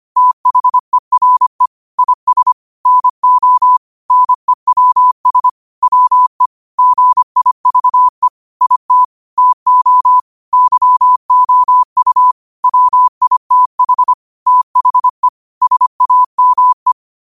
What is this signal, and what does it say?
Morse code, heard as a keyed tone, about 25 wpm, 1000 Hz. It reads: THERE IS NO NEWS WE GIVE IT TO YOU WITH THE SAME
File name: qotdfiles / 25wpm